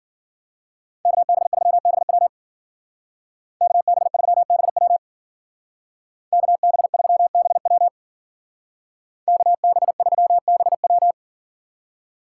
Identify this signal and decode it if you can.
Morse code, heard as a keyed tone, about 52 wpm, 700 Hz. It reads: KB3BW KB3BW KB3BW KB3BW